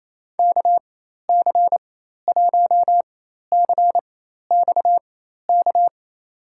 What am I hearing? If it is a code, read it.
Morse code, heard as a keyed tone, about 28 wpm, spaced as for 13 wpm, 700 Hz. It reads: KC1CXK
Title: USA Call Signs - 6 Characters - 4x Character Spacing - 28wpm